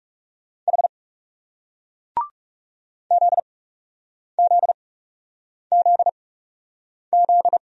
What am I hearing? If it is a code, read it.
Morse code, heard as a keyed tone, about 41 wpm, spaced as for 8 wpm, 700 Hz. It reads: H ZZZZ